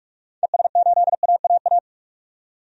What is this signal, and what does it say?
Morse code, heard as a keyed tone, about 45 wpm, 700 Hz. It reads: ES8AAA
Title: DXCC-100 Call Signs - 6 Characters - 45wpm